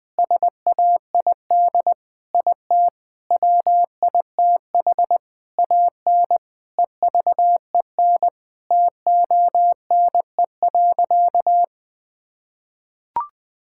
Morse code, heard as a keyed tone, about 20 wpm, 700 Hz. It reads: SAID IT WITH AN EVEN TONE.
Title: Sentences from Top 1600 Words 20wpm